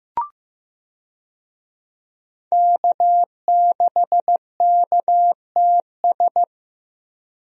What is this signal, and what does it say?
Morse code, heard as a keyed tone, about 15 wpm, 700 Hz. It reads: K6KTS
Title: USA Call Signs - 5 Characters - Repeated 2x - 15wpm